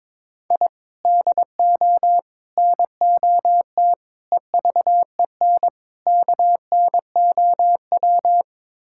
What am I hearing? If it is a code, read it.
Morse code, heard as a keyed tone, about 22 wpm, 700 Hz. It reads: I DO NOT EVEN KNOW
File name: practice / 22wpm / Build A Phrase - 5-Word Phrases - 22wpm